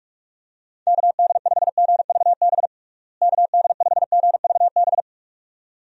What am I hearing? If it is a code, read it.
Morse code, heard as a keyed tone, about 47 wpm, 700 Hz. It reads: KD5GVB KD5GVB
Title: USA Call Signs - 6 Characters - Repeated 2x - 45wpm